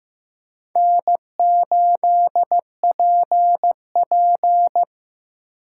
Morse code, heard as a keyed tone, about 15 wpm, 700 Hz. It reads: N8PP
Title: USA Call Signs - 4 Characters - 2x Character Spacing - 15wpm